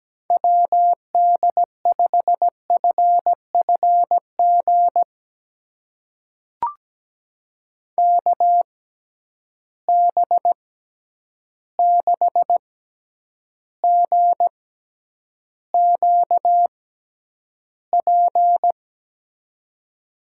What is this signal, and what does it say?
Morse code, heard as a keyed tone, about 17 wpm, spaced as for 9 wpm, 700 Hz. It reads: WD5FFG KB6GQP
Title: USA Call Signs - 6 Characters - 6x Character Spacing - 17wpm